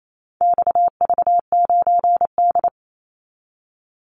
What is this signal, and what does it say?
Morse code, heard as a keyed tone, about 28 wpm, 700 Hz. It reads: XV9D
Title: Call Signs - 28wpm